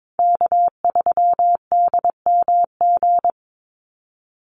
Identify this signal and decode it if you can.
Morse code, heard as a keyed tone, about 22 wpm, 700 Hz. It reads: K3DMG
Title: Call Signs - 22wpm